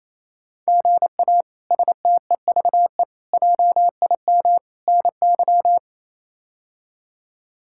Morse code, heard as a keyed tone, about 28 wpm, 700 Hz. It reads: GA STEVE JIM NY
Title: K1USN SST QSO Practice - 28wpm